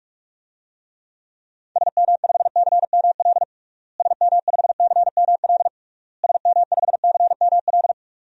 Morse code, heard as a keyed tone, about 46 wpm, 700 Hz. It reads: SM5CML SM5CML SM5CML
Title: DXCC-100 Call Signs - 6 Characters - Repeated 3x - 45wpm